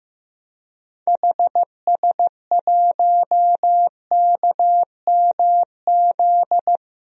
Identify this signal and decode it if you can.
Morse code, heard as a keyed tone, about 16 wpm, 700 Hz. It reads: HS1KMZ
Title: DXCC-100 Call Signs - 6 Characters - Repeated 2x - 15wpm